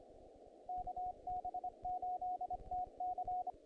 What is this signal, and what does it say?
Morse code, heard as a keyed tone, about 29 wpm, 730 Hz, with radio noise. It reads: KB8TC
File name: contests / naqp-cw-k5tr-op-k5pi- -1900